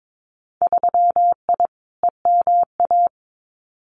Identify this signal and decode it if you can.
Morse code, heard as a keyed tone, about 22 wpm, 700 Hz. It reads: 3I EMA
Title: Winter Field Day Exchanges Encore 22wpm